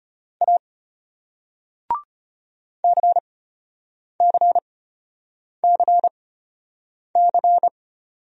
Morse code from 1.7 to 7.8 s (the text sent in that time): CCCC